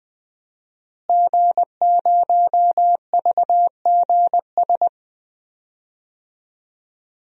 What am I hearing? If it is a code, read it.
Morse code, heard as a keyed tone, about 20 wpm, 700 Hz. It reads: G0VGS
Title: DXCC-100 Call Signs - 5 Characters - Repeated 3x - 20wpm